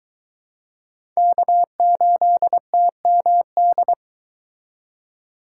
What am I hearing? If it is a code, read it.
Morse code, heard as a keyed tone, about 23 wpm, 700 Hz. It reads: K8TMD